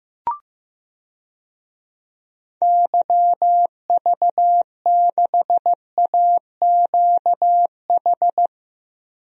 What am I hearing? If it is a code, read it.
Morse code, heard as a keyed tone, about 15 wpm, 700 Hz. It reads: YV6AQH
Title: DXCC-100 Call Signs - 6 Characters - Repeated 2x - 15wpm